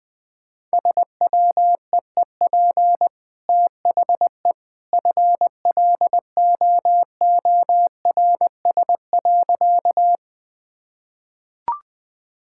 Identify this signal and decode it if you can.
Morse code, heard as a keyed tone, about 20 wpm, 700 Hz. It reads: SWEEP THE FLOORS.
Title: Easy Sentences with Hints - 20wpm